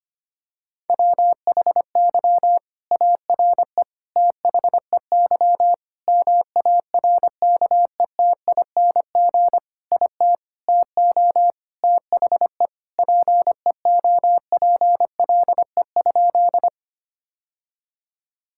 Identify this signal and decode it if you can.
Morse code, heard as a keyed tone, about 25 wpm, 700 Hz. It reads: WHY ARE THEY MARKETING IT TO THE PEOPLE?
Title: Sentences from Top 1000 Words - Review Part 5 - 25wpm